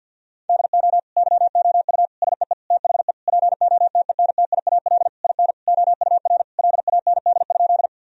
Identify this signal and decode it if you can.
Morse code, heard as a keyed tone, about 50 wpm, 700 Hz. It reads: DO YOU SEE THE POTENTIAL IN OUR LAND?
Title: Sentences from Top 1000 Words - Review Part 4 - 50wpm